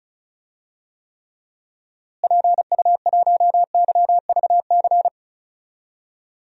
Morse code, heard as a keyed tone, about 36 wpm, 700 Hz. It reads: PU1YVC
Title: DXCC-100 Call Signs - 6 Characters - Repeated 3x - 35wpm